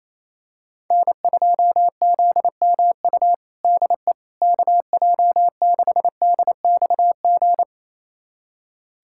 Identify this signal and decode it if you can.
Morse code, heard as a keyed tone, about 28 wpm, 700 Hz. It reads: N2ZMU DE KJ6DXG
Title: USA Callsign DE Callsign 28wpm